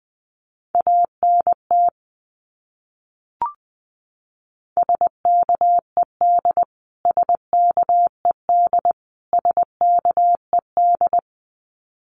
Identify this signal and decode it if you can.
Morse code, heard as a keyed tone, about 19 wpm, 700 Hz. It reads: ANT SKED SKED SKED
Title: Abbreviations Repeated 3x - 20wpm